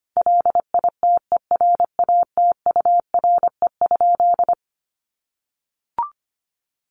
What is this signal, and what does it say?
Morse code, heard as a keyed tone, about 25 wpm, 700 Hz. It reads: LITERATURE?
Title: Sentences from Top 1000 Words 25wpm